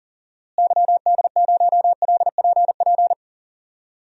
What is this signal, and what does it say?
Morse code, heard as a keyed tone, about 40 wpm, 700 Hz. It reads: YD0LPP
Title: DXCC-100 Call Signs - 6 Characters - Repeated 2x - 40wpm